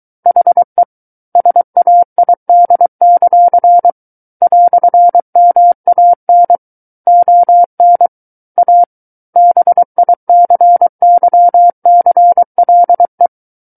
Morse code, heard as a keyed tone, about 23 wpm, 700 Hz. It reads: HE SAID; "MAN ON A BICYCLE
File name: Book One / 14-Chapter 14 - 23 WPM